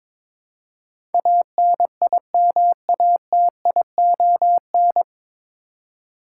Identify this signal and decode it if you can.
Morse code, heard as a keyed tone, about 21 wpm, 700 Hz. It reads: ANIMATION
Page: Morse Code Ninja practice 22wpm